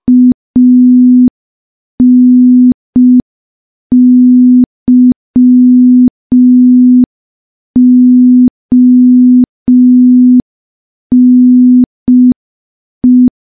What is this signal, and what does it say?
Morse code, heard as a keyed tone, about 5 wpm, 250 Hz. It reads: ANYONE
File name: morse / morse-training-05WPM